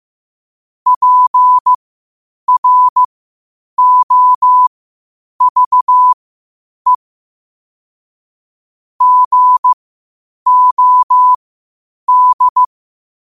Morse code, heard as a keyed tone, about 15 wpm, spaced as for 8 wpm, 1000 Hz. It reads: PROVE GOD
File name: qotdfiles / 8wpm